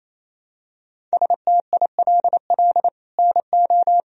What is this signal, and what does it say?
Morse code, heard as a keyed tone, about 28 wpm, 700 Hz. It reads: STILL NO
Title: Sentences from Top 1000 Words - Review Part 1 - 28wpm